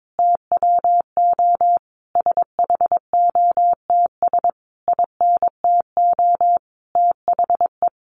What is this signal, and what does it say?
Morse code, heard as a keyed tone, about 22 wpm, 700 Hz. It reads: TWO SHOTS INTO THE
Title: A Sound Of Thunder 22wpm